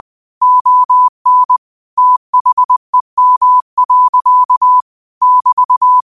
Morse code, heard as a keyed tone, about 20 wpm, 1000 Hz. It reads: ON THEM. <BT>
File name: qotdfiles / 20wpm